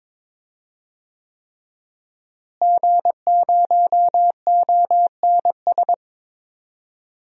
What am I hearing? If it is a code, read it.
Morse code, heard as a keyed tone, about 22 wpm, 700 Hz. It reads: G0ONS
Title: DXCC-100 Call Signs - 5 Characters - Repeated 3x - 22wpm